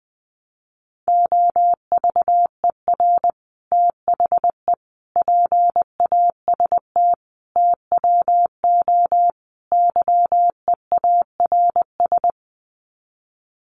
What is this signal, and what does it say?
Morse code, heard as a keyed tone, about 20 wpm, 700 Hz. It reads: OVER THE PAST TWO YEARS
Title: Sets of 5 Words Encore 20wpm